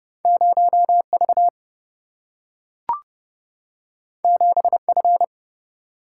Morse code, heard as a keyed tone, about 30 wpm, 700 Hz. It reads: 0 V 7 F
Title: Single-Letter Single-Number 30wpm